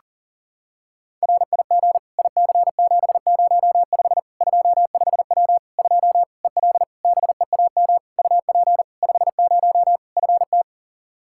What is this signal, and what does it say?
Morse code, heard as a keyed tone, about 40 wpm, 700 Hz. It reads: RIG IC705 25W 2 EL BEAM UP 50 FT
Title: Practice QSO - Exchange 2 - Info - 40wpm